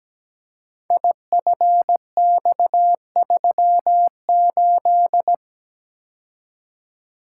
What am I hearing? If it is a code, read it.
Morse code, heard as a keyed tone, about 17 wpm, 700 Hz. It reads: IFX38
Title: Code Groups - 5 Letter:Numbers - 17wpm